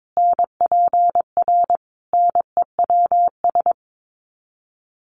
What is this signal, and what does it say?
Morse code, heard as a keyed tone, about 22 wpm, 700 Hz. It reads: NPR NEWS
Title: Sets of 5 Words 22wpm